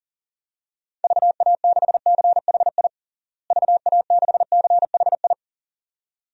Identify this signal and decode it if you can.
Morse code, heard as a keyed tone, about 40 wpm, 700 Hz. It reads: VA6CHI VA6CHI
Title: DXCC-100 Call Signs - 6 Characters - Repeated 2x - 40wpm